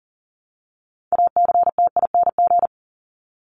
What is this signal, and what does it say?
Morse code, heard as a keyed tone, about 40 wpm, 700 Hz. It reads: ACTING
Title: ING Suffix - Top 100 Words with Base Words 40wpm